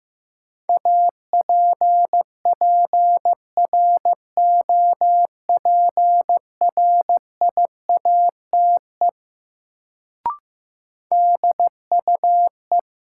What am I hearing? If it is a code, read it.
Morse code, heard as a keyed tone, about 15 wpm, 700 Hz. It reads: APPROPRIATE DUE
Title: Sentences from Top 900 Words - New Word Drill - 15wpm